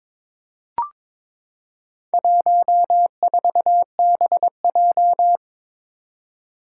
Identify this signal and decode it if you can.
Morse code, heard as a keyed tone, about 22 wpm, 700 Hz. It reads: 14BJ
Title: Four Challenging Letter-Number Combinations 22wpm